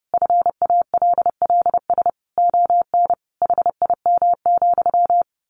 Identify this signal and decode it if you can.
Morse code, heard as a keyed tone, about 30 wpm, 700 Hz. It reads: FALLS ON HIM,
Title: A Sound Of Thunder 30wpm